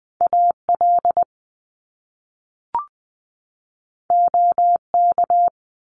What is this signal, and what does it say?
Morse code, heard as a keyed tone, about 20 wpm, 700 Hz. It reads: AL OK
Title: Top Two Letter Combinations in Words 20wpm